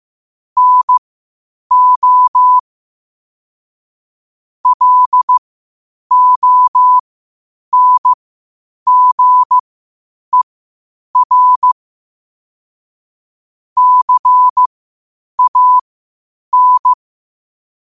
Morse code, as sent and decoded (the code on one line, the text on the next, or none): -. --- / .-.. --- -. --. .-. / -.-. .- -.
NO LONGER CAN